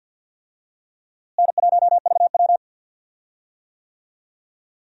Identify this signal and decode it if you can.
Morse code, heard as a keyed tone, about 53 wpm, 700 Hz. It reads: N1VW
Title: USA Call Signs - 4 Characters - Repeated 2x - 50wpm